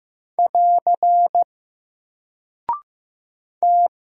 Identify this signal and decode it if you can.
Morse code, heard as a keyed tone, about 15 wpm, 700 Hz. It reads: <AR> T
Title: Single Character (Letter:Number:Punctuation:Prosign) 15wpm